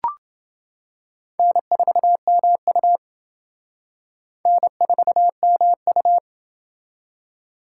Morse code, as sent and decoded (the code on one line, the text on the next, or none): -. ....- -- ..- / -. ....- -- ..-
N4MU N4MU